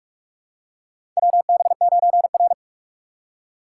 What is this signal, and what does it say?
Morse code, heard as a keyed tone, about 44 wpm, 690 Hz. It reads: WB9R
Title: USA Call Signs - 4 Characters - 2x Character Spacing - 45wpm